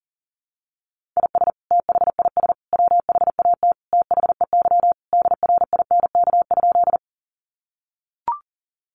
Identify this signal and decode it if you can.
Morse code, heard as a keyed tone, about 40 wpm, 700 Hz. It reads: IS THIS WHAT THEY DRINK?
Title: Sentences from Top 1500 Words 40wpm